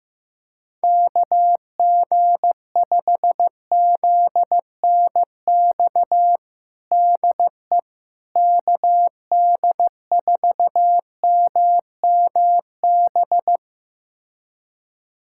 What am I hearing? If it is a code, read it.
Morse code, heard as a keyed tone, about 15 wpm, 700 Hz. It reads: KG5ZNX DE KD4MMB
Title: USA Callsign DE Callsign 15wpm